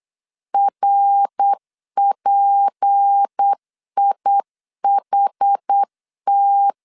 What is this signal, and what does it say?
Morse code, heard as a keyed tone, about 8 wpm, 800 Hz. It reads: RPIHT
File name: 8wpm